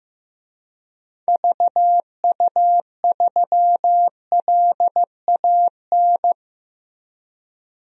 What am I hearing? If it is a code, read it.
Morse code, heard as a keyed tone, about 15 wpm, 700 Hz. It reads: VU3LAN
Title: DXCC-100 Call Signs - 6 Characters - Repeated 3x - 15wpm